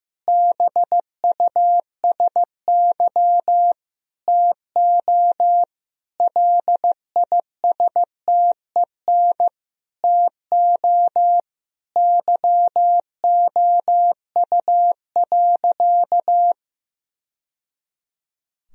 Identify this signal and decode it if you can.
Morse code, heard as a keyed tone, about 15 wpm, 700 Hz. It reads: BUSY TO LISTEN TO YOU.
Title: Sentences from Top 1000 Words - Review Part 3 - 15wpm